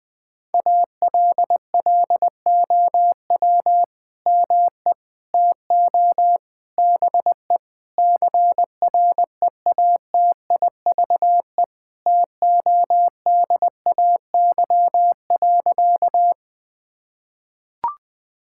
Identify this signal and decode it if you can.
Morse code, heard as a keyed tone, about 20 wpm, 700 Hz. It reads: ALLOW ME TO BE CREATIVE TODAY.
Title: Sentences from Top 1000 Words - Review Part 5 - 20wpm